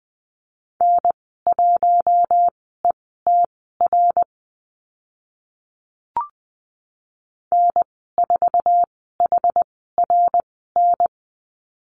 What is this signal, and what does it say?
Morse code, heard as a keyed tone, about 20 wpm, spaced as for 14 wpm, 700 Hz. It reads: N1ETR N4HRN
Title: 088 Lesson 15g - Number 5 - Call Signs 20wpm